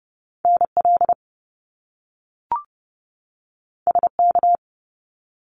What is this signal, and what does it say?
Morse code, heard as a keyed tone, about 30 wpm, 700 Hz. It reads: NL SK
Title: Canadian Provinces and Territories 30wpm